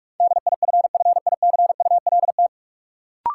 Morse code, heard as a keyed tone, about 46 wpm, 700 Hz. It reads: DIFFICULT
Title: Sentences from Top 600 Words - New Word Drill - 45wpm